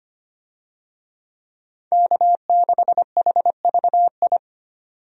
Morse code, heard as a keyed tone, about 25 wpm, 690 Hz. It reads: K6HVI
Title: USA Call Signs - 5 Characters - 25wpm